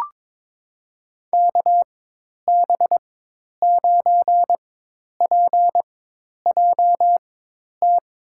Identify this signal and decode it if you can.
Morse code, heard as a keyed tone, about 22 wpm, spaced as for 10 wpm, 700 Hz. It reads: KB9PJT
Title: USA Call Signs - 6 Characters - 4x Character Spacing - 22wpm